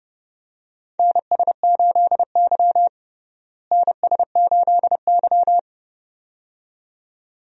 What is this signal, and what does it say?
Morse code, heard as a keyed tone, about 30 wpm, 700 Hz. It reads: NS8Y NS8Y
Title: USA Call Signs - 4 Characters - Repeated 2x - 30wpm